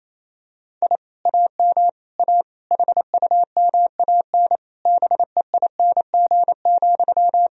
Morse code, heard as a keyed tone, about 28 wpm, 700 Hz. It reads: I AM A HUMAN BEING,
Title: Sentences from Top 1000 Words - Review Part 3 - 28wpm